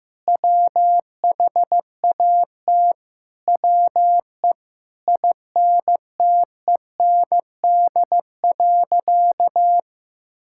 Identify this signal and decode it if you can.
Morse code, heard as a keyed tone, about 15 wpm, 700 Hz. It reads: WHAT WE INTEND.
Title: Sentences from Top 1600 Words 15wpm